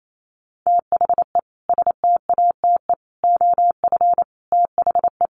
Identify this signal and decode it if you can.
Morse code, heard as a keyed tone, about 28 wpm, 700 Hz. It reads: THE STATE OF THE
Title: Sentences from Top 1400 Words 28wpm